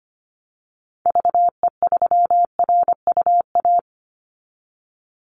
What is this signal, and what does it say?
Morse code, heard as a keyed tone, about 25 wpm, 700 Hz. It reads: VE3RUA
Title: Call Signs - 25wpm